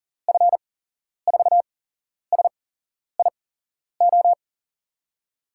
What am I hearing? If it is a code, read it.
Morse code, heard as a keyed tone, about 37 wpm, spaced as for 11 wpm, 700 Hz. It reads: F4SIO
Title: USA Call Signs - 6 Characters - 8x Character Spacing - 40wpm